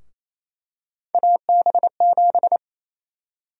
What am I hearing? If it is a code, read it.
Morse code, heard as a keyed tone, about 27 wpm, 700 Hz. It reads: AB7
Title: USA Call Sign Prefixes - Repeated 2x - 28wpm